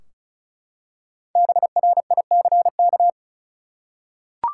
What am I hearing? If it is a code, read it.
Morse code, heard as a keyed tone, about 34 wpm, 700 Hz. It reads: BRICK
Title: Five Letter Words - 35wpm